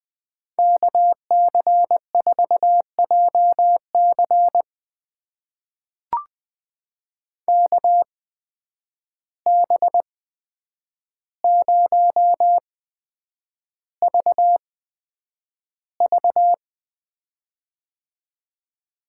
Morse code, heard as a keyed tone, about 20 wpm, spaced as for 9 wpm, 700 Hz. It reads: KC4JC KB0VV
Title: USA Call Signs - 5 Characters - 8x Character Spacing - 20wpm